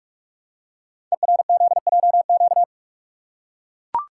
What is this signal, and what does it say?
Morse code, heard as a keyed tone, about 44 wpm, 700 Hz. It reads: ERZJQ
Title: Code Groups - 5 Letters - 45wpm